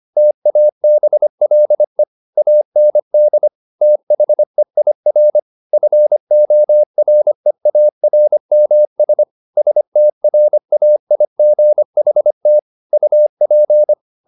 Morse code, heard as a keyed tone, about 25 wpm, 600 Hz. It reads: TABLE AND THEIR FOREARMS STRAIGHT UP
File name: the old man and the sea part 1 25wpm